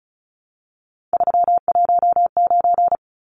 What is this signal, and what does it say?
Morse code, heard as a keyed tone, about 34 wpm, 700 Hz. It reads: 319
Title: Numbers - 3-Digit 35wpm